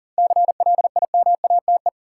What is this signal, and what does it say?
Morse code, heard as a keyed tone, about 40 wpm, 700 Hz. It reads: CLIMATE
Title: Sentences from Top 1600 Words - New Word Drill - 40wpm